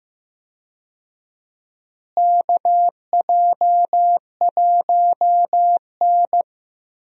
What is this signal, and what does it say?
Morse code, heard as a keyed tone, about 15 wpm, 700 Hz. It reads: KJ1N
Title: USA Call Signs - 4 Characters - Repeated 3x - 15wpm